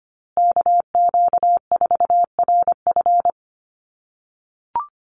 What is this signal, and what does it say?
Morse code, heard as a keyed tone, about 25 wpm, 700 Hz. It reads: KQ4RF
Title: USA Call Signs Repeated 3x 25wpm